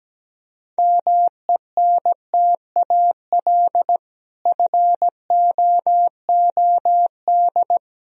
Morse code, heard as a keyed tone, about 17 wpm, 700 Hz. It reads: MENTAL FOOD
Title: Sentences from Top 1000 Words - Review Part 4 - 17wpm